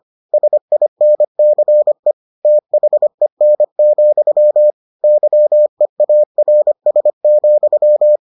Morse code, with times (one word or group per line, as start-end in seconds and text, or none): 0.3-2.1 SINCE
2.5-4.7 THEN,
5.0-8.2 YEARS,